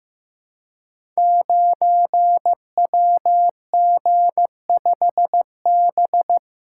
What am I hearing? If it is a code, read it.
Morse code, heard as a keyed tone, about 15 wpm, 700 Hz. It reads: 9WG5B
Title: Code Groups - 5 Letter:Numbers - 15wpm